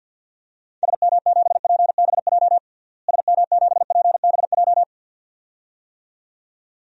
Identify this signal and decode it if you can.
Morse code, heard as a keyed tone, about 53 wpm, 700 Hz. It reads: SM7PBJ SM7PBJ
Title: DXCC-100 Call Signs - 6 Characters - Repeated 2x - 50wpm